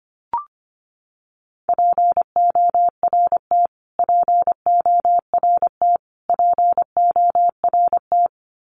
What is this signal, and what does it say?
Morse code, heard as a keyed tone, about 25 wpm, 700 Hz. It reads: PORT PORT PORT
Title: Top 500 Words Repeated 25wpm